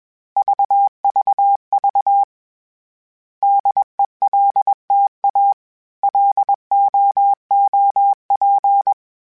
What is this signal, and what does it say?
Morse code, heard as a keyed tone, about 21 wpm, 800 Hz. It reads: VVV DELTA LOOP